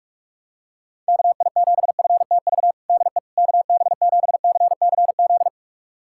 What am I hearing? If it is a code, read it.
Morse code, heard as a keyed tone, about 43 wpm, 700 Hz. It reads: KI7FTV DE KB7CCZ
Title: USA Callsign DE Callsign - 6 Characters - 45wpm